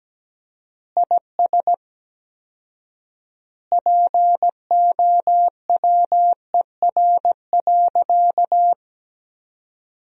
Morse code, heard as a keyed tone, about 17 wpm, 700 Hz. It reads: IS POWER.
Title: 4x Word-Spacing - Sentences from Top 500 Words - 17wpm